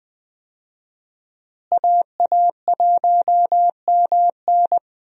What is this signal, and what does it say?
Morse code, heard as a keyed tone, about 21 wpm, 700 Hz. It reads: AA1MN
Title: USA Call Signs - 5 Characters - Repeated 3x - 20wpm